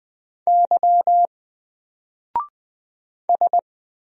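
Morse code, heard as a keyed tone, about 20 wpm, 700 Hz. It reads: Y S
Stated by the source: Single letter and number Morse drill at 20 wpm, from N4LCD.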